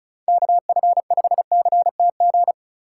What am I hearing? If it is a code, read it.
Morse code, heard as a keyed tone, about 43 wpm, 700 Hz. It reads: KF5CTG
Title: Speed Racing - USA 2x3 Call Signs 35wpm